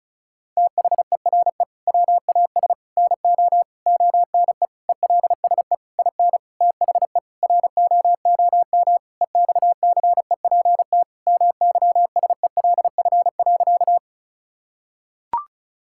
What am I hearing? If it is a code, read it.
Morse code, heard as a keyed tone, about 36 wpm, 700 Hz. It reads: THERE WAS NO ONE ELSE IN THE ROOM EXCEPT MYSELF.
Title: Sentences from Top 1600 Words 35wpm